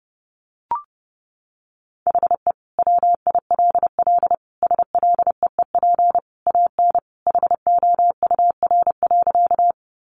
Morse code, heard as a keyed tone, about 30 wpm, 700 Hz. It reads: HE WILL SLEEP AN HOUR.
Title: Sentences from Top 500 Words 30wpm